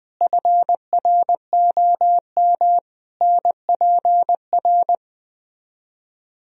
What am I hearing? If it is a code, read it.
Morse code, heard as a keyed tone, about 20 wpm, 700 Hz. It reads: FROM NPR
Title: Speed-Racing - Sets of 5 Words 17wpm